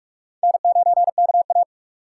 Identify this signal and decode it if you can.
Morse code, heard as a keyed tone, about 45 wpm, 700 Hz. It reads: N9KA
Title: USA Call Signs - 2x Character Spacing - 45wpm